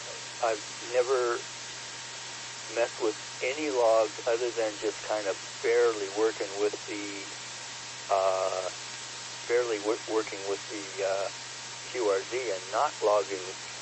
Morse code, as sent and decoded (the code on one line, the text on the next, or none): none